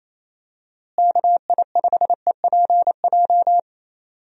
Morse code, heard as a keyed tone, about 28 wpm, 700 Hz. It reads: KI5EPJ
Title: USA Call Signs - 6 Characters - Repeated 2x - 28wpm